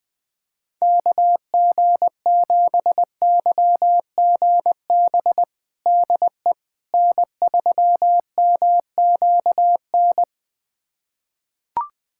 Morse code, heard as a keyed tone, about 20 wpm, 690 Hz. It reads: KG7YGB DE N3MQN
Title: USA Callsign DE Callsign 20wpm